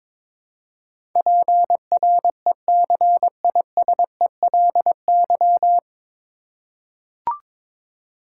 Morse code, heard as a keyed tone, about 22 wpm, 700 Hz. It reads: PRECISELY
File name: 9 Letter Words - First 3 Letters Given - 22wpm